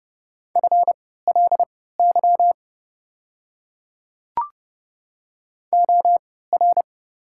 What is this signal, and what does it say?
Morse code, heard as a keyed tone, about 30 wpm, spaced as for 17 wpm, 690 Hz. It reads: FLY OR
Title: 3x Character Spacing - Top 500 Words - 30wpm